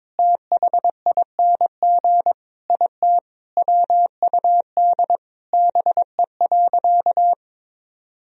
Morse code, heard as a keyed tone, about 23 wpm, 700 Hz. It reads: THING IT WUD BE.
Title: Sentences from Top 200 Words and Abbreviations 22wpm